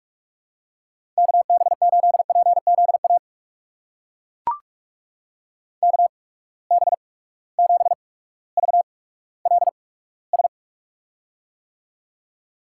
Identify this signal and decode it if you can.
Morse code, heard as a keyed tone, about 42 wpm, spaced as for 19 wpm, 700 Hz. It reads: KB8PZA KB7VLS
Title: USA Call Signs - 6 Characters - 8x Character Spacing - 45wpm